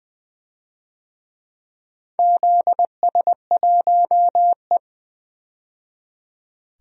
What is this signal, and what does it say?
Morse code, heard as a keyed tone, about 20 wpm, 700 Hz. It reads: ZS1E
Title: DXCC-100 Call Signs - 4 Characters - 20wpm